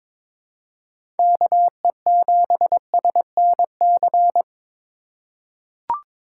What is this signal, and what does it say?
Morse code, heard as a keyed tone, about 22 wpm, 700 Hz. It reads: KE7SNC
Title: USA Call Signs - 6 Characters - 2x Character Spacing - 22wpm